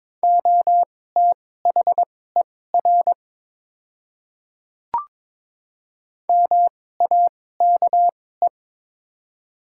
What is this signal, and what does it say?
Morse code, heard as a keyed tone, about 22 wpm, spaced as for 16 wpm, 690 Hz. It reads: OTHER MAKE
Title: 2x Character Spacing - Top 100 Words - 22wpm